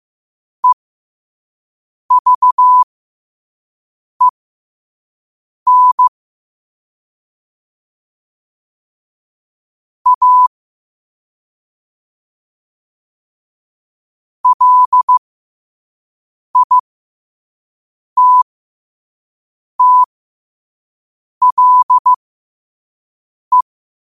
Morse code, heard as a keyed tone, about 15 wpm, spaced as for 5 wpm, 1000 Hz. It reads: EVEN A LITTLE
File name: qotdfiles / 5wpm